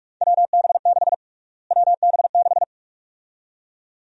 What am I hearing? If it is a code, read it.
Morse code, heard as a keyed tone, about 46 wpm, 700 Hz. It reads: WB6 WB6
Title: USA Call Sign Prefixes - Repeated 2x - 45wpm